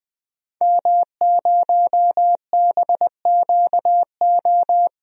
Morse code, heard as a keyed tone, about 20 wpm, 700 Hz. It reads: M0BQO
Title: DXCC-100 Call Signs - 5 Characters - Repeated 2x - 20wpm